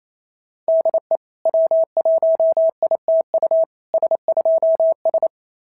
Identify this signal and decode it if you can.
Morse code, heard as a keyed tone, about 25 wpm, 650 Hz. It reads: DE W1ITU S2S
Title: SOTA QSO Practice 28wpm